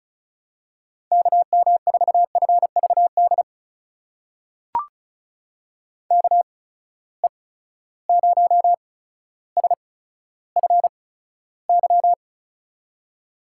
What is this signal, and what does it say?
Morse code, heard as a keyed tone, about 36 wpm, spaced as for 15 wpm, 700 Hz. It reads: KM4FVD KE0SFY
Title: USA Call Signs - 6 Characters - 8x Character Spacing - 35wpm